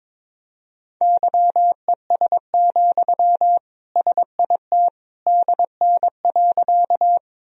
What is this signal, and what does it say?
Morse code, heard as a keyed tone, about 22 wpm, 700 Hz. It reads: YES, SIT DN.
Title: Sentences from Top 500 Words and Abbreviations 22wpm